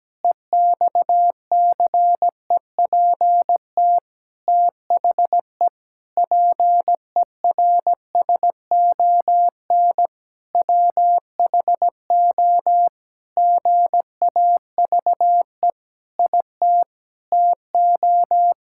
Morse code, heard as a keyed tone, about 17 wpm, 700 Hz. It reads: EXCEPT THE PERSON WHO GAVE IT TO
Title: Sentences from Top 1600 Words 17wpm